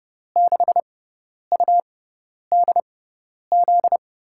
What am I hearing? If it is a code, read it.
Morse code, heard as a keyed tone, about 30 wpm, spaced as for 10 wpm, 700 Hz. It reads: 6UDZ
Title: USA Call Signs - 6 Characters - 6x Character Spacing - 30wpm